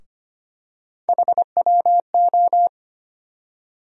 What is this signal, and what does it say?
Morse code, heard as a keyed tone, about 25 wpm, 700 Hz. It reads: HWO
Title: USA Call Sign Postfixes - 3 Letters 25wpm